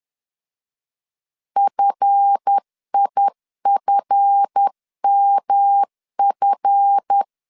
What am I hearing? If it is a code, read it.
Morse code, heard as a keyed tone, about 10 wpm, 800 Hz. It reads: FIFMF
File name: LEZIONI / 7 GF9Z / 7 GF 10wpm 2514